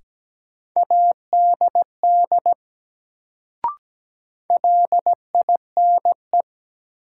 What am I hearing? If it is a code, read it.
Morse code, heard as a keyed tone, about 17 wpm, 700 Hz. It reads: ADD LINE
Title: Sentences from Top 1000 Words - Review Part 1 - New Word Drill - 17wpm